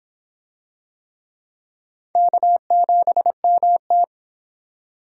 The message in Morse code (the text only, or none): K7MT